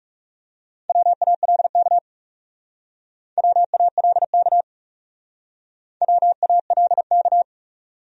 Morse code, heard as a keyed tone, about 37 wpm, 700 Hz. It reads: WALK WALK WALK
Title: Speed-Racing Top 300 Words 30wpm